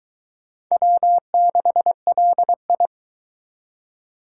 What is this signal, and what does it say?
Morse code, heard as a keyed tone, about 22 wpm, 700 Hz. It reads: W6LI